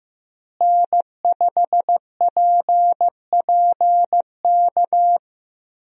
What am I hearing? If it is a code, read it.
Morse code, heard as a keyed tone, about 15 wpm, 700 Hz. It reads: N5PPK
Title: USA Call Signs - 5 Characters - Repeated 3x - 15wpm